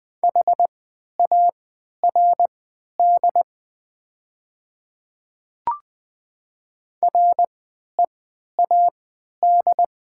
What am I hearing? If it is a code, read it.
Morse code, heard as a keyed tone, about 20 wpm, spaced as for 11 wpm, 700 Hz. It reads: HARD READ
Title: 3x Character Spacing - Top 500 Words - 20wpm